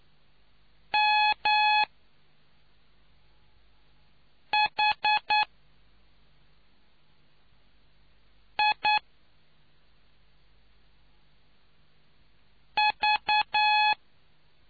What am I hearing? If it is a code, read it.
Morse code, heard as a keyed tone, about 9 wpm, spaced as for 2 wpm, 850 Hz. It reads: MHIV